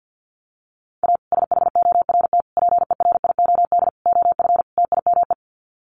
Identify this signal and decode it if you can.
Morse code, heard as a keyed tone, about 50 wpm, 700 Hz. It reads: A SHORT PERIOD OF TIME
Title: Sets of 5 Words Encore 50wpm